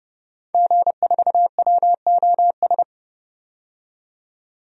Morse code, heard as a keyed tone, about 30 wpm, 700 Hz. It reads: G4WOS
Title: DXCC-100 Call Signs - 5 Characters - 30wpm